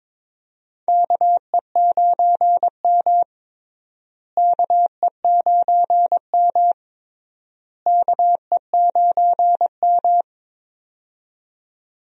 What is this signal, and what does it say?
Morse code, heard as a keyed tone, about 22 wpm, 700 Hz. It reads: KE9M KE9M KE9M
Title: USA Call Signs - 4 Characters - Repeated 3x - 22wpm